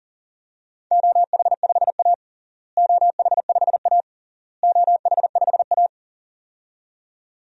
Morse code, heard as a keyed tone, about 40 wpm, 700 Hz. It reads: OH5A OH5A OH5A
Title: DXCC-100 Call Signs - 4 Characters - Repeated 3x - 40wpm